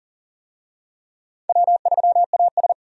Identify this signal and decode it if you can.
Morse code, heard as a keyed tone, about 40 wpm, 700 Hz. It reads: W3AS
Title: USA Call Signs - 4 Characters - 40wpm